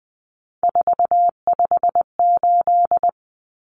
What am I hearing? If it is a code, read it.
Morse code, heard as a keyed tone, about 20 wpm, 700 Hz. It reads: 458
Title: Common Cut Numbers - 3-Digit 20wpm